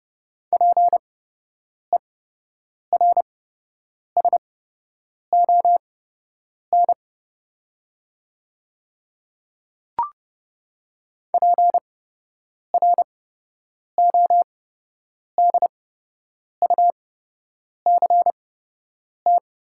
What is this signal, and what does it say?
Morse code, heard as a keyed tone, about 31 wpm, spaced as for 8 wpm, 700 Hz. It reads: PERSON PRODUCT
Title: 8x Character Spacing - Top 500 Words - 30wpm